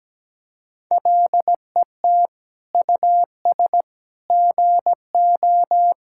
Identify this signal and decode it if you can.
Morse code, heard as a keyed tone, about 17 wpm, 700 Hz. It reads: LET US GO
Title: Sentences from Top 1000 Words - Review Part 5 - 17wpm